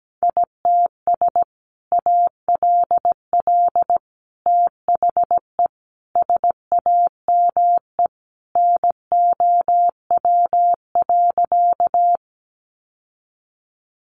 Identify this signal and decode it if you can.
Morse code, heard as a keyed tone, about 17 wpm, 700 Hz. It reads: ITS ALL THE SAME NOW.
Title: Sentences from Top 600 Words 17wpm